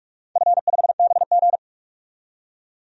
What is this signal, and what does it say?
Morse code, heard as a keyed tone, about 48 wpm, 700 Hz. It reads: F5BG
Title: DXCC-100 Call Signs - 4 Characters - 45wpm